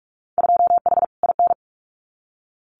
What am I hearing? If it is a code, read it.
Morse code, heard as a keyed tone, about 44 wpm, 700 Hz. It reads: 2H IN
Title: Winter Field Day Exchanges Encore 45wpm